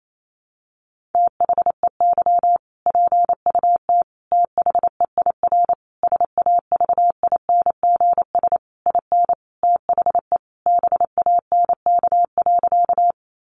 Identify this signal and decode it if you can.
Morse code, heard as a keyed tone, about 28 wpm, 700 Hz. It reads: THEY PUT THEIR SAVINGS IN THE BANK.
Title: Sentences from Top 1200 Words 28wpm